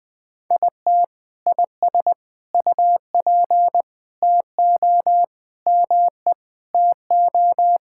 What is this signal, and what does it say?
Morse code, heard as a keyed tone, about 20 wpm, 700 Hz. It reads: IT IS UP TO ME TO
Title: Sentences with 2-Word Phrases - 20wpm